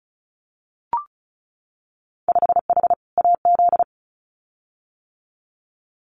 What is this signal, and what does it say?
Morse code, heard as a keyed tone, about 36 wpm, 700 Hz. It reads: 5H AZ
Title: Winter Field Day Exchanges Encore 35wpm